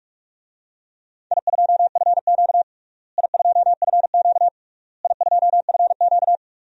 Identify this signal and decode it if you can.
Morse code, heard as a keyed tone, about 46 wpm, 700 Hz. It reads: I2FQ I2FQ I2FQ
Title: DXCC-100 Call Signs - 4 Characters - Repeated 3x - 45wpm